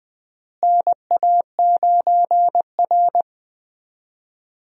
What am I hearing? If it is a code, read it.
Morse code, heard as a keyed tone, about 20 wpm, 700 Hz. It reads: NA9R
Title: USA Call Signs - 4 Characters - Repeated 2x - 20wpm